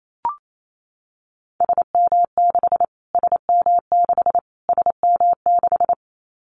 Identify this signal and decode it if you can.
Morse code, heard as a keyed tone, about 28 wpm, 700 Hz. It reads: SM6 SM6 SM6
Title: Callsign Prefixes for Top 25 DXCC - Repeated 3x - 28wpm